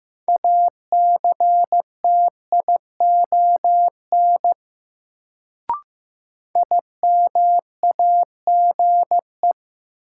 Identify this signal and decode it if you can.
Morse code, heard as a keyed tone, about 15 wpm, 700 Hz. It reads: ACTION IMAGE
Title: Sentences from Top 700 Words - New Word Drill - 15wpm